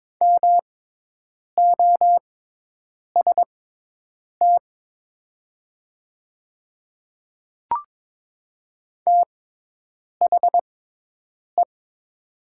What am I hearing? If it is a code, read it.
Morse code, heard as a keyed tone, about 22 wpm, spaced as for 8 wpm, 700 Hz. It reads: MOST THE
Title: 6x Character Spacing - Top 200 Words - 22wpm